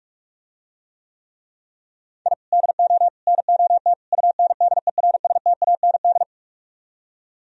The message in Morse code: I DO NOT UNDERSTAND

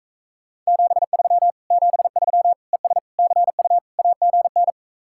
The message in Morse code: 73 73 ES CU AGN

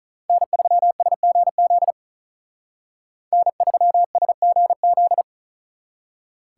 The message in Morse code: N3SGZ N3SGZ

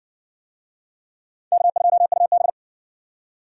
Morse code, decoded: K2UB

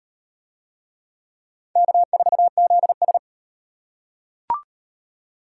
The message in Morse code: K4ZS

